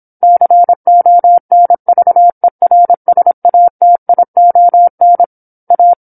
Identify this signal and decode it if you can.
Morse code, heard as a keyed tone, about 27 wpm, 700 Hz. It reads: CONVERSATION A
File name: 08-Chapter-08 26 WPM